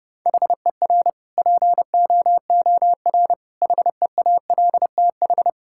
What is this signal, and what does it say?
Morse code, heard as a keyed tone, about 30 wpm, 700 Hz. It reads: HER POOR HEALTH
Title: Sentences from Top 1000 Words - Review Part 6 - 30wpm